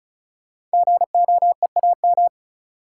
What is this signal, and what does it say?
Morse code, heard as a keyed tone, about 36 wpm, 700 Hz. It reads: GOEAM
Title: Code Groups - 5 Letters - 35wpm